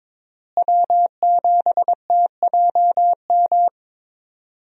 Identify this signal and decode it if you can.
Morse code, heard as a keyed tone, about 22 wpm, 700 Hz. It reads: W7TJM
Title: USA Call Signs - 5 Characters - Repeated 3x - 22wpm